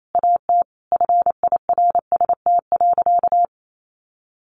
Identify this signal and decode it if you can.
Morse code, heard as a keyed tone, about 28 wpm, 700 Hz. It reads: AT FIRST.
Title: Sentences from Top 1400 Words 28wpm